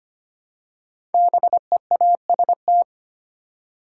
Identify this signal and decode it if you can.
Morse code, heard as a keyed tone, about 25 wpm, 700 Hz. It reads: BEAST
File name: 5 Letter Words - First 3 Letters Given - 25wpm